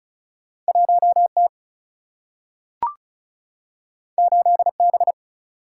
Morse code, heard as a keyed tone, about 36 wpm, 700 Hz. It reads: 1T 8B
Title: Single-Letter Single-Number 35wpm